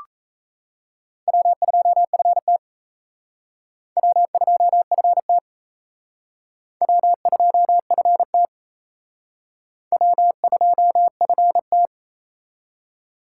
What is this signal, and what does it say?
Morse code, heard as a keyed tone, about 36 wpm, 700 Hz. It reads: W2FT W2FT W2FT W2FT